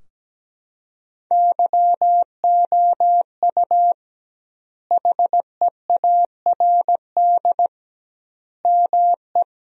Morse code, heard as a keyed tone, about 17 wpm, 700 Hz. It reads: YOU HEARD ME
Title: 2x Word-Spacing - Sentences from Top 500 Words - 17wpm